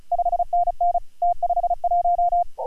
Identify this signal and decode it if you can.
Morse code, heard as a keyed tone, about 36 wpm, 690 Hz. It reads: 5NN T51